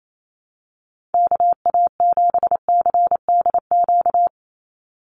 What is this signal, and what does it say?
Morse code, heard as a keyed tone, about 28 wpm, 700 Hz. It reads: KA7CDQ
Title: USA Call Signs 28wpm